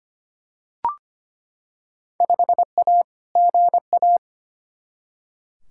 Morse code, heard as a keyed tone, about 25 wpm, 700 Hz. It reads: 5A GA
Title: Field Day Exchanges - 25wpm